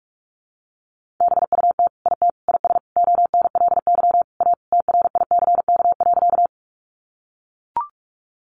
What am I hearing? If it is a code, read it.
Morse code, heard as a keyed tone, about 44 wpm, 700 Hz. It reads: BUT IT IS ONLY A TRICK.
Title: Sentences from Top 1500 Words 45wpm